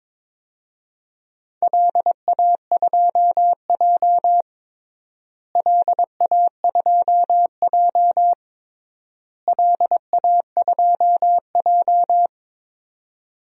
Morse code, heard as a keyed tone, about 22 wpm, 700 Hz. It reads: LA2J LA2J LA2J
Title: DXCC-100 Call Signs - 4 Characters - Repeated 3x - 22wpm